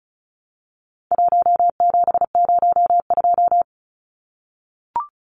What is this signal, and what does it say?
Morse code, heard as a keyed tone, about 34 wpm, 700 Hz. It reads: 1702
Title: Numbers - 4-Digit 35wpm